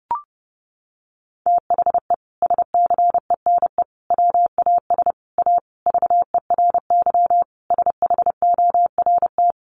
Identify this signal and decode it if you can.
Morse code, heard as a keyed tone, about 30 wpm, 700 Hz. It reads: THE SCENE WAS A VERY SHORT
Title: Sentences from Top 1100 Words 30wpm